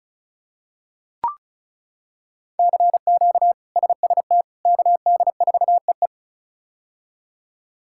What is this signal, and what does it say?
Morse code, heard as a keyed tone, about 36 wpm, 700 Hz. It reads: CQ SST KD4EE
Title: K1USN SST QSO Practice - 35wpm